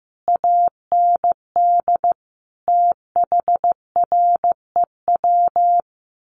Morse code, heard as a keyed tone, about 15 wpm, 700 Hz. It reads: AND THREW
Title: A Sound Of Thunder 15wpm